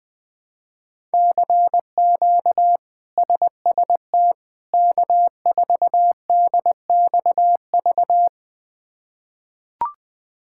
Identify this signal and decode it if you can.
Morse code, heard as a keyed tone, about 21 wpm, 700 Hz. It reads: CQ SST K4DXV
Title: K1USN SST QSO Practice - 20wpm